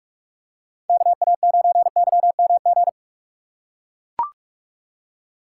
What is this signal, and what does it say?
Morse code, heard as a keyed tone, about 43 wpm, 700 Hz. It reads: KA9YMG
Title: K1USN SST QSO Practice - 45wpm